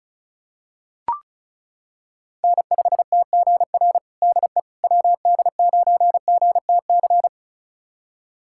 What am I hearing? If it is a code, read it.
Morse code, heard as a keyed tone, about 36 wpm, 700 Hz. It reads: N5TGR DE WD9GTC
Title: USA Callsign DE Callsign 35wpm